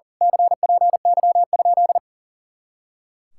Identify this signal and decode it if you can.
Morse code, heard as a keyed tone, about 41 wpm, 700 Hz. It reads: CPY?
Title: Speed-Racing Common QSO Phrases - 40wpm